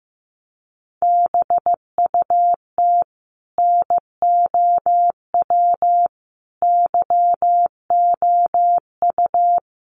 Morse code, heard as a keyed tone, about 15 wpm, 700 Hz. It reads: BUT NOW YOU
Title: Sentences from Top 1000 Words 15wpm